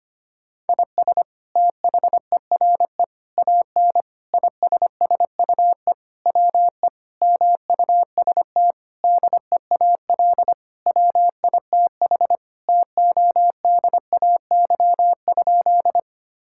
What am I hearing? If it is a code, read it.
Morse code, heard as a keyed tone, about 25 wpm, 700 Hz. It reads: IS THERE AN ISSUE WE MUST DEAL WITH TODAY?
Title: Sentences from Top 1000 Words - Review Part 6 - 25wpm